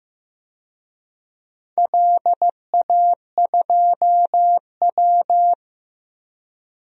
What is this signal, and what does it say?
Morse code, heard as a keyed tone, about 16 wpm, 700 Hz. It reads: LA2W
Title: DXCC-100 Call Signs - 4 Characters - Repeated 3x - 15wpm